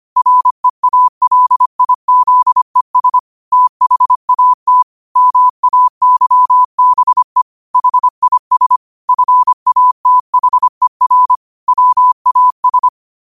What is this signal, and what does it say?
Morse code, heard as a keyed tone, about 25 wpm, 1000 Hz. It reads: REALIZES THAT MAYBE HIS FATHER WAS
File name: qotdfiles / 25wpm